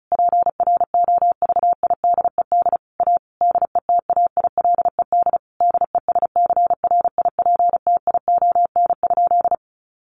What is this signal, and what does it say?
Morse code, heard as a keyed tone, about 34 wpm, 700 Hz. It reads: PROVIDED A DETAILED DESCRIPTION?
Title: Sentences from Top 1500 Words 35wpm